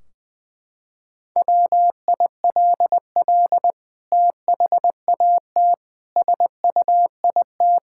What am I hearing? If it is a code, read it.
Morse code, heard as a keyed tone, about 20 wpm, 700 Hz. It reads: WILL THAT SUIT
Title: Sentences from Top 1600 Words 20wpm